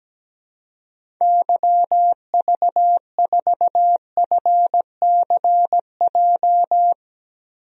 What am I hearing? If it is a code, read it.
Morse code, heard as a keyed tone, about 17 wpm, 700 Hz. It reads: YV4FCJ
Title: DXCC-100 Call Signs - 6 Characters - Repeated 3x - 17wpm